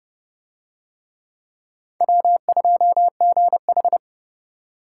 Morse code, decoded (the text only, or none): W2GH